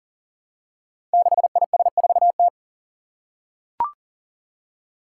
Morse code, heard as a keyed tone, about 40 wpm, 700 Hz. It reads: 6IS4T
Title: Five Challenging Letter-Number Combinations 40wpm